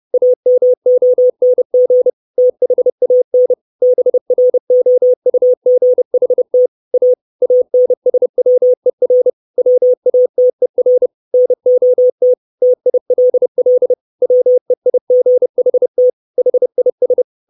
Morse code, heard as a keyed tone, about 30 wpm, 500 Hz. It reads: AMONG THAN BROUGHT A ANSWER WATER NOT TILL WEIGHT HIS
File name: 632 Words 30WPM